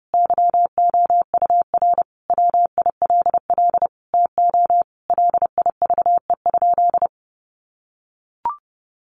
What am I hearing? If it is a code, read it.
Morse code, heard as a keyed tone, about 30 wpm, 700 Hz. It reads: YOUR WILL TO LIVE?
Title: Sentences from Top 1500 Words 30wpm